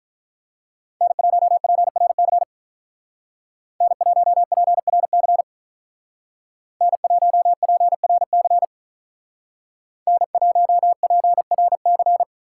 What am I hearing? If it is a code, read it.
Morse code, heard as a keyed tone, about 42 wpm, 700 Hz. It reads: N1PRC N1PRC N1PRC N1PRC
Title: Speed Racing - USA Call Signs 35wpm